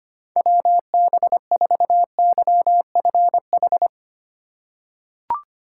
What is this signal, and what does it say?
Morse code, heard as a keyed tone, about 25 wpm, 700 Hz. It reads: WB4YFH
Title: USA Call Signs - 6 Characters - 4x Character Spacing - 25wpm